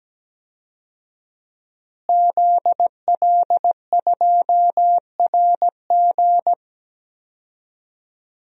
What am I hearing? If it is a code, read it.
Morse code, heard as a keyed tone, about 17 wpm, 700 Hz. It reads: ZL2RG
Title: DXCC-100 Call Signs - 5 Characters - Repeated 3x - 17wpm